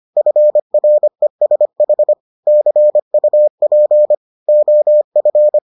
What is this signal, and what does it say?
Morse code, heard as a keyed tone, about 25 wpm, 600 Hz. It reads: FRESH CUP OF